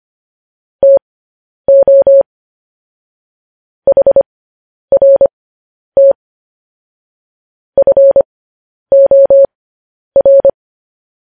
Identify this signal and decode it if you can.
Morse code, heard as a keyed tone, about 25 wpm, spaced as for 10 wpm, 550 Hz. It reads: TO HRT FOR
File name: podcast / News headlines-1x-25x10